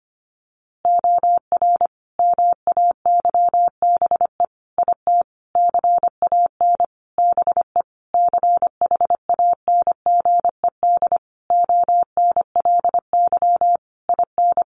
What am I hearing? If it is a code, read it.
Morse code, heard as a keyed tone, about 25 wpm, 700 Hz. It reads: OR MAYBE IT CAN BE CHANGED ONLY IN
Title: A Sound Of Thunder 25wpm